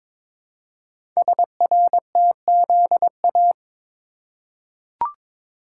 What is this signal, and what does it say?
Morse code, heard as a keyed tone, about 22 wpm, 700 Hz. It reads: SRTZA
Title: Code Groups - 5 Letters - 22wpm